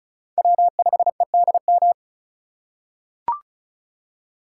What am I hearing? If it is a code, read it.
Morse code, heard as a keyed tone, about 36 wpm, 690 Hz. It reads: W5EDM
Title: USA Call Signs - 5 Characters - 2x Character Spacing - 35wpm